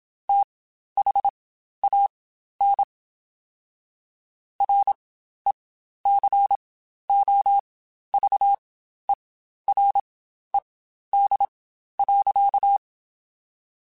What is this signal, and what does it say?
Morse code, heard as a keyed tone, about 27 wpm, spaced as for 12 wpm, 800 Hz. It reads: THAN RECOVERED.
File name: Quotes Spaced / Quotes Spaced-mp3s / Character sp @25